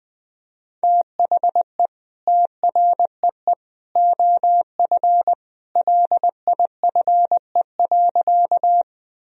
Morse code, heard as a keyed tone, about 20 wpm, 700 Hz. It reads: THE TREE OF LIFE.
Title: Sentences from Top 1000 Words - Review Part 1 - 20wpm